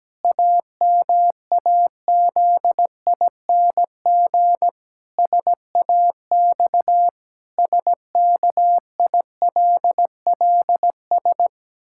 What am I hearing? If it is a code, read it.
Morse code, heard as a keyed tone, about 17 wpm, 700 Hz. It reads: AMAZING SAX SKILLS
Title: Pangram Challenge v2 - 17wpm